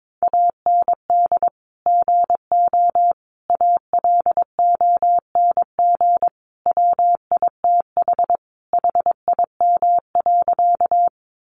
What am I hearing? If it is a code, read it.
Morse code, heard as a keyed tone, about 22 wpm, 700 Hz. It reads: AND GO ALONG WITH HIM.
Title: Sentences from Top 700 Words 22wpm